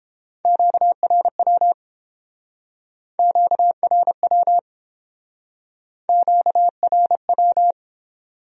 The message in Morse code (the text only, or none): QRW QRW QRW